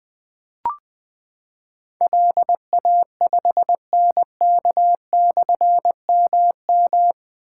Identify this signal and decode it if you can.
Morse code, heard as a keyed tone, about 20 wpm, 700 Hz. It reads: LA5NK/MM
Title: Difficult Call Signs 20wpm